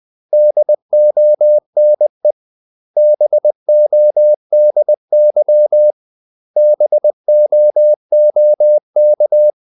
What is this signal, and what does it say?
Morse code, heard as a keyed tone, about 20 wpm, 600 Hz. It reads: DONE BODY BOOK
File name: CW / Practice / 500MostCommonEnglishWords20